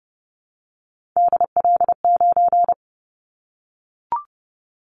Morse code, heard as a keyed tone, about 30 wpm, 700 Hz. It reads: DL9
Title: Callsign Prefixes for Top 25 DXCC - Repeated 3x - 30wpm